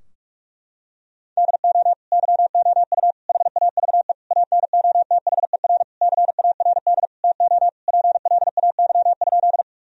Morse code, heard as a keyed tone, about 46 wpm, 700 Hz. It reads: DO YOU HAVE ANOTHER CARD TO PLAY?
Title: Sentences from Top 1000 Words - Review Part 4 - 45wpm